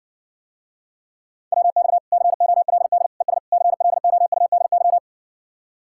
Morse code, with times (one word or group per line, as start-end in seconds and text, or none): 1.5-2.0 WX
2.1-3.1 COLD
3.2-3.4 ES
3.5-5.0 CLOUDY